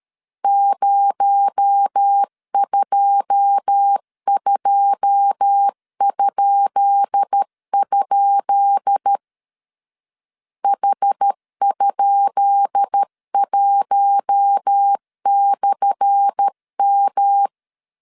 Morse code, heard as a keyed tone, about 13 wpm, 800 Hz. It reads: 022?? H?1/M
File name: LEZIONI / 10 SIGNS / 10B INT 12wpm 2414